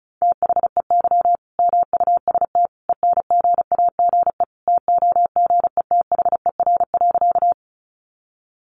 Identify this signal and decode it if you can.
Morse code, heard as a keyed tone, about 34 wpm, 700 Hz. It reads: THEY MUST ENGAGE TOGETHER.
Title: Sentences from Top 1500 Words 35wpm